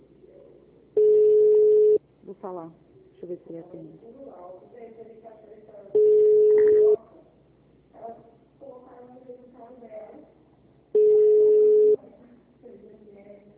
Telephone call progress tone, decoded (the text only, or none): ringback tone